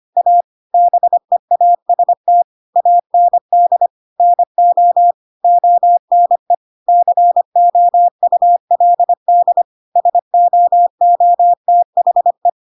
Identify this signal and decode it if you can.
Morse code, heard as a keyed tone, about 25 wpm, 700 Hz. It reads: A BEAST AND NO ONE COULD SOOTHE
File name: Pinocchio / Pinocchio ch2 25 25 ik1hge